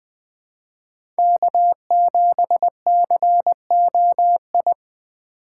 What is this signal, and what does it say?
Morse code, heard as a keyed tone, about 20 wpm, 700 Hz. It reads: K7COI
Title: USA Call Signs - 5 Characters - 2x Character Spacing - 20wpm